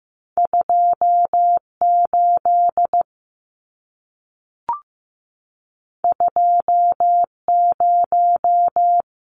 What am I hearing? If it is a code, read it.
Morse code, heard as a keyed tone, about 15 wpm, 700 Hz. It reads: 28 20
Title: Numbers - 2-Digit 15wpm